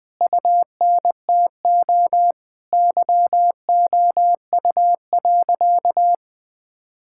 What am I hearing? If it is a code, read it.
Morse code, heard as a keyed tone, about 20 wpm, 700 Hz. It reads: UNTO YOU.
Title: Sentences with 9-Word Phrases - Spoken Beginning - 20wpm